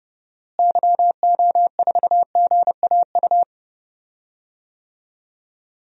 Morse code, heard as a keyed tone, about 30 wpm, 700 Hz. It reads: YO4GAU
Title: DXCC-100 Call Signs - 6 Characters - Repeated 3x - 30wpm